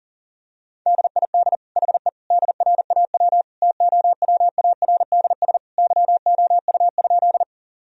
Morse code, heard as a keyed tone, about 40 wpm, 700 Hz. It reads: DID HE DRAW TOWARDS YOU?
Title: Sentences from Top 1000 Words - Review Part 2 - 40wpm